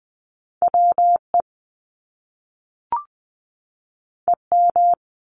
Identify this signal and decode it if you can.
Morse code, heard as a keyed tone, about 20 wpm, 700 Hz. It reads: WE EM
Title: Top Two Letter Combinations in Words 20wpm